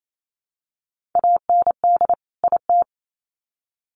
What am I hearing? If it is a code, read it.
Morse code, heard as a keyed tone, about 28 wpm, 700 Hz. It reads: AND IT
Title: Sets of 2 Words 28wpm